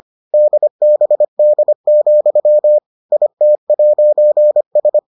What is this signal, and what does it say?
Morse code, heard as a keyed tone, about 25 wpm, 600 Hz. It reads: DBD, IT'S